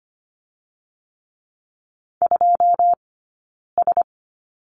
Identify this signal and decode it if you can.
Morse code, heard as a keyed tone, about 25 wpm, spaced as for 9 wpm, 690 Hz. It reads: 2S
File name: Lesson 15d - Number 5 - Two Characters 25wpm